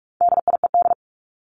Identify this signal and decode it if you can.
Morse code, heard as a keyed tone, about 44 wpm, 700 Hz. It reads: DIED
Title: ED Suffix - Top 100 Words with Base Words 45wpm